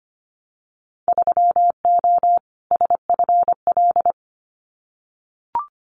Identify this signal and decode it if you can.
Morse code, heard as a keyed tone, about 25 wpm, 700 Hz. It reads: 3O SFL
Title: Winter Field Day Exchanges 25wpm